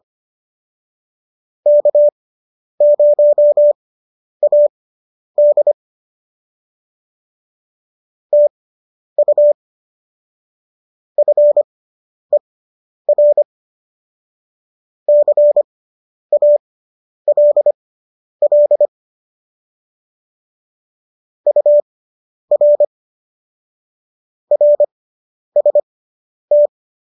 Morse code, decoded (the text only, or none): K0AD TU FER CALL UR RST